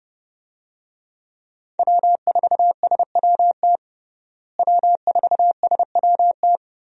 Morse code, heard as a keyed tone, about 30 wpm, 700 Hz. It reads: W4SWT W4SWT
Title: USA Call Signs - 5 Characters - Repeated 2x - 30wpm